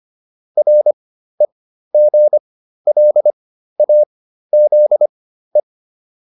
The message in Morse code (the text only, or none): REGLAZE